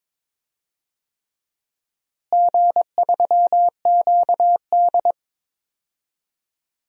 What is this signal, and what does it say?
Morse code, heard as a keyed tone, about 22 wpm, 700 Hz. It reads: G3QD
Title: DXCC-100 Call Signs - 4 Characters - Repeated 3x - 22wpm